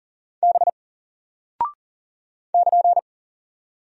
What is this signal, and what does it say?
Morse code, heard as a keyed tone, about 39 wpm, 700 Hz. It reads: B <KN>
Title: Single Character (Letter:Number:Punctuation:Prosign) Alt 40wpm